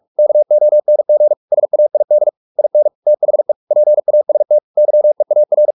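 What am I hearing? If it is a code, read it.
Morse code, heard as a keyed tone, about 44 wpm, 600 Hz. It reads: KONG SAID IN THE PAST YEAR
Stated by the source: Morse practice at 45 wpm, the novel Never by Ken Follett in English, from CW med Gnister.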